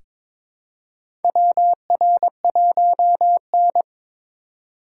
Morse code, heard as a keyed tone, about 22 wpm, 700 Hz. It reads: WR1N
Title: USA Call Signs - 4x Character Spacing - 22wpm